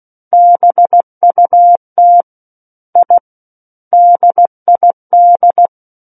Morse code, heard as a keyed tone, about 16 wpm, 700 Hz. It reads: BUT I DID
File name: Book One / 07-Chapter 7 - 16 WPM